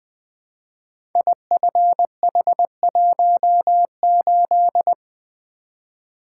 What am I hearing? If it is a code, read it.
Morse code, heard as a keyed tone, about 20 wpm, 700 Hz. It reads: IFH18
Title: Code Groups - 5 Letter:Numbers - 20wpm